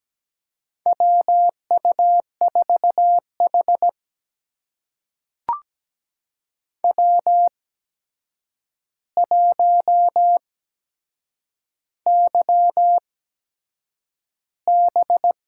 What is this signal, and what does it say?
Morse code, heard as a keyed tone, about 17 wpm, spaced as for 7 wpm, 700 Hz. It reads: WU4H W1YB
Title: USA Call Signs - 4 Characters - 8x Character Spacing - 17wpm